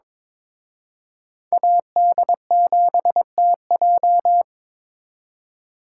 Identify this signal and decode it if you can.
Morse code, heard as a keyed tone, about 26 wpm, 700 Hz. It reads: AD7TJ
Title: Speed Racing - USA 2x2 Call Signs 22wpm